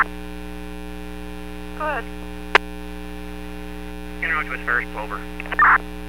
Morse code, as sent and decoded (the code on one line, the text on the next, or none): none